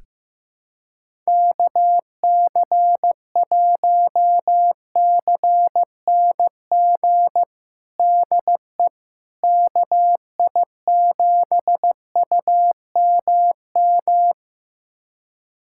Morse code, heard as a keyed tone, about 16 wpm, 700 Hz. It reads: KC1CNG DE KI7UMM
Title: USA Callsign DE Callsign 15wpm